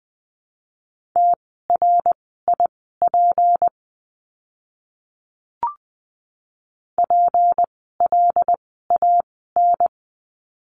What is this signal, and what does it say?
Morse code, heard as a keyed tone, about 20 wpm, spaced as for 14 wpm, 700 Hz. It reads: TRIP PLAN
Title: 153 Lesson 25f - Letter p - Words 20wpm